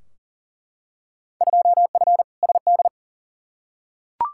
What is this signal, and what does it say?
Morse code, heard as a keyed tone, about 37 wpm, 700 Hz. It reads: 2F SD
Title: Field Day Exchanges Encore - 40wpm